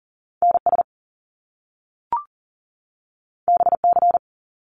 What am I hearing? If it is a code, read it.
Morse code, heard as a keyed tone, about 40 wpm, 700 Hz. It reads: NS BC
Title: Canadian Provinces and Territories 40wpm